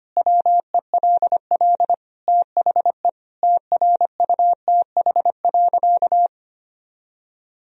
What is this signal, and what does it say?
Morse code, heard as a keyed tone, about 25 wpm, 700 Hz. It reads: WELL THE TRUTH.
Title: Sentences from Top 1000 Words - Review Part 3 - 25wpm